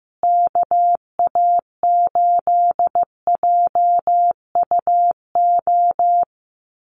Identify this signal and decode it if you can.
Morse code, heard as a keyed tone, about 15 wpm, 700 Hz. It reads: KA8JUO
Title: USA Call Signs Repeated 2x 15wpm